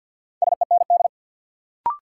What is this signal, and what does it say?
Morse code, heard as a keyed tone, about 52 wpm, 700 Hz. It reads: SEND
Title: Sentences from Top 1500 Words - New Word Drill Alt - 50wpm